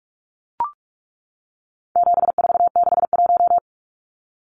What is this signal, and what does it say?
Morse code, heard as a keyed tone, about 44 wpm, 700 Hz. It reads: 7461
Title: Numbers - 4-Digit 45wpm